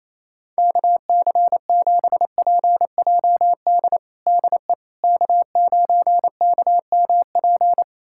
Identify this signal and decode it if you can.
Morse code, heard as a keyed tone, about 28 wpm, 700 Hz. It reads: KC7PJD DE K9KMP
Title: USA Callsign DE Callsign 28wpm